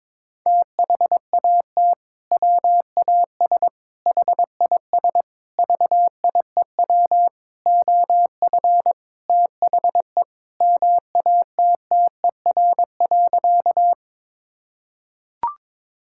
Morse code, heard as a keyed tone, about 22 wpm, 700 Hz. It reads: THAT WAS HIS VIEW OF THE MATTER.
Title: Sentences from Top 1000 Words - Review Part 6 - 22wpm